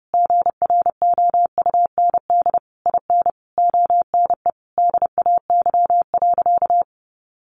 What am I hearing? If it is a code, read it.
Morse code, heard as a keyed tone, about 30 wpm, 700 Hz. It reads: GROUND IN ONE DAY.
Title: Sentences from Top 500 Words 30wpm